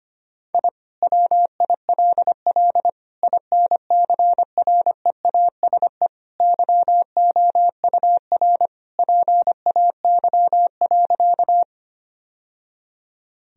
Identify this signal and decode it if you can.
Morse code, heard as a keyed tone, about 25 wpm, 700 Hz. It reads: I WILL INCREASE YOUR PAY.
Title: Sentences from Top 1000 Words - Review Part 4 - 25wpm